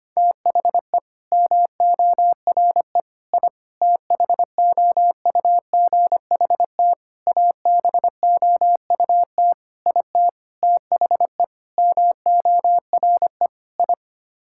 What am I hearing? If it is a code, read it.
Morse code, heard as a keyed tone, about 25 wpm, 700 Hz. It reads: THE MORE I THOUGHT ABOUT IT THE MORE I
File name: Sets of 9 Words - 25wpm